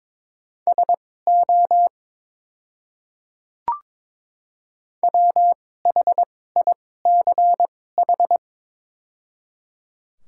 Morse code, heard as a keyed tone, about 22 wpm, spaced as for 16 wpm, 690 Hz. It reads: SO WHICH
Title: 2x Character Spacing - Top 100 Words - 22wpm